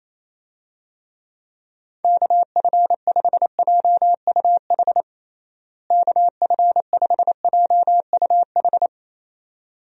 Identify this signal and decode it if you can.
Morse code, heard as a keyed tone, about 28 wpm, 700 Hz. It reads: KF5JUH KF5JUH
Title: USA Call Signs - 6 Characters - Repeated 2x - 28wpm